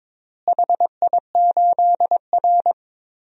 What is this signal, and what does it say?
Morse code, heard as a keyed tone, about 22 wpm, 700 Hz. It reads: HI8R
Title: DXCC-100 Call Signs - 4 Characters - Repeated 2x - 22wpm